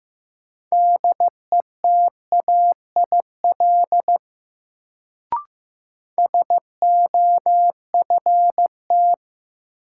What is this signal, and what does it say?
Morse code, heard as a keyed tone, about 15 wpm, 700 Hz. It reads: DETAIL SOFT
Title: Sentences from Top 1500 Words - New Word Drill Alt - 15wpm